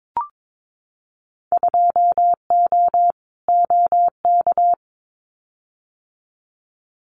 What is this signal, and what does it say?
Morse code, heard as a keyed tone, about 22 wpm, 700 Hz. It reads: 2O OK
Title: Winter Field Day Exchanges 22wpm